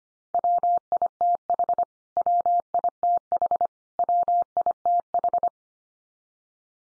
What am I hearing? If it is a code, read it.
Morse code, heard as a keyed tone, about 25 wpm, 700 Hz. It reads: WITH WITH WITH
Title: Top 100 Words 25wpm x3